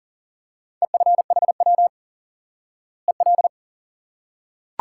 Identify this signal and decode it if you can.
Morse code, heard as a keyed tone, about 40 wpm, 700 Hz. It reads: EFHW EL
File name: IQE Course 3 - Lesson 89 - YAGI - Two Elements - 40wpm